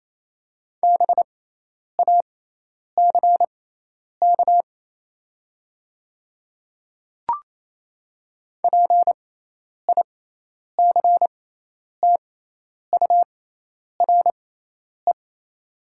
Morse code, heard as a keyed tone, about 27 wpm, spaced as for 10 wpm, 690 Hz. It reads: BACK PICTURE
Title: 6x Character Spacing - Top 300 Words - 28wpm